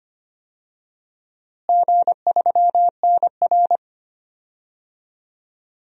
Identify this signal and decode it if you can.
Morse code, heard as a keyed tone, about 25 wpm, 700 Hz. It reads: G3NR
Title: DXCC-100 Call Signs - 4 Characters - Repeated 2x - 25wpm